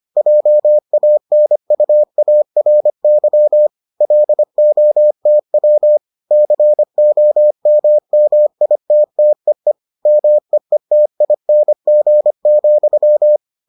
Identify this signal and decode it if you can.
Morse code, heard as a keyed tone, about 25 wpm, 600 Hz. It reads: JANUARY LOTW COMMITTEE MEETING,